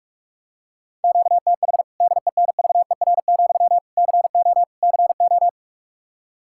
Morse code, heard as a keyed tone, about 46 wpm, 700 Hz. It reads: QTH DENVER, CO CO
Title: K1USN SST QSO Practice 45wpm